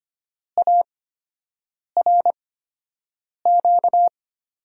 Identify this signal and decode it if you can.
Morse code, heard as a keyed tone, about 25 wpm, spaced as for 7 wpm, 700 Hz. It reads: ARQ
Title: USA Call Signs - 6 Characters - 8x Character Spacing - 25wpm